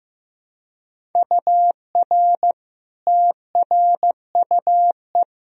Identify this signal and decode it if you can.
Morse code, heard as a keyed tone, about 16 wpm, 700 Hz. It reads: UR TRUE
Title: Sentences from Top 400 Words and Abbreviations 15wpm